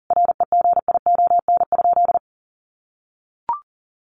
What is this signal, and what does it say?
Morse code, heard as a keyed tone, about 40 wpm, 700 Hz. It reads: REGION?
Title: Sentences from Top 1500 Words 40wpm